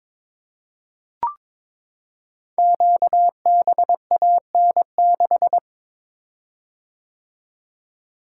Morse code, heard as a keyed tone, about 22 wpm, 690 Hz. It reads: QBAN6
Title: Code Groups - 5 Letter:Numbers - 22wpm